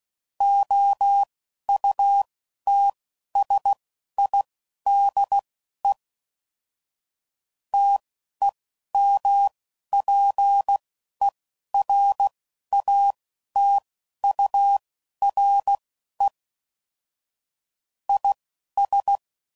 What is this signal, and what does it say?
Morse code, heard as a keyed tone, about 16 wpm, spaced as for 11 wpm, 800 Hz. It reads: OUTSIDE TEMPERATURE IS
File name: CW Code Groups @10WPM / Weather Sets / Weather 5 1